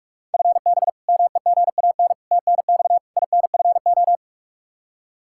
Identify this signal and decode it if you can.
Morse code, heard as a keyed tone, about 41 wpm, 700 Hz. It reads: FB MEGAN TNX INFO
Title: Practice QSO - Exchange 3 - More Info - 45wpm